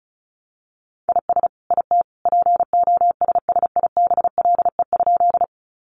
Morse code, heard as a keyed tone, about 34 wpm, 700 Hz. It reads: IS IT POSSIBLE?
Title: Sentences from Top 500 Words 35wpm